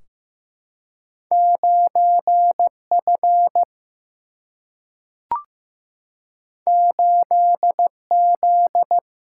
Morse code, heard as a keyed tone, about 15 wpm, 700 Hz. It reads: 9F 8Z
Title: Single-Letter Single-Number 15wpm